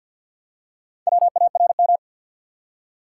Morse code, heard as a keyed tone, about 49 wpm, 700 Hz. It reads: WARM
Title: Four Letter Words - 50wpm